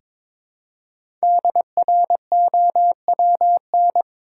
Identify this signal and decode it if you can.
Morse code, heard as a keyed tone, about 22 wpm, 690 Hz. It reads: DROWN
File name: 5 Letter Words - First 3 Letters Given - 22wpm